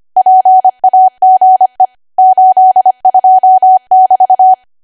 Morse code, heard as a keyed tone, about 25 wpm, 750 Hz. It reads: PAGE 82 <BT>
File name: cwa / W1AW-Level3 / 25WPM